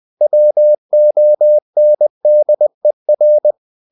Morse code, heard as a keyed tone, about 20 wpm, 600 Hz. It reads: WONDER
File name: the old man and the sea part 2 20wpm